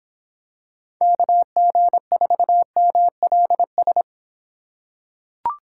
Morse code, heard as a keyed tone, about 25 wpm, 700 Hz. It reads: KG4MLS